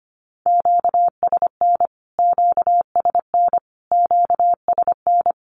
Morse code, heard as a keyed tone, about 25 wpm, 700 Hz. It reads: QSN QSN QSN
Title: Q-Codes Repeated 3x 25wpm